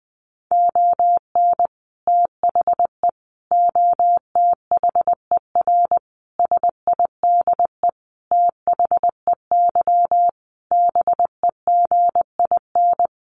Morse code, heard as a keyed tone, about 20 wpm, 700 Hz. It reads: ON THE OTHER SIDE THEY BEGIN
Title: Sentences from Top 1200 Words 20wpm